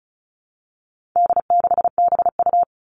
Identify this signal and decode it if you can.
Morse code, heard as a keyed tone, about 34 wpm, 700 Hz. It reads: D6BU
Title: All Cut Numbers - 4-Digit 35wpm